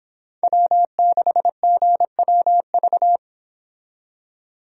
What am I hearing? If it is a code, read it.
Morse code, heard as a keyed tone, about 26 wpm, 700 Hz. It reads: W6GWV